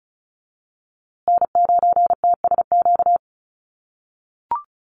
Morse code, heard as a keyed tone, about 34 wpm, 700 Hz. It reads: N9TSQ
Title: USA Call Signs Repeated 2x 35wpm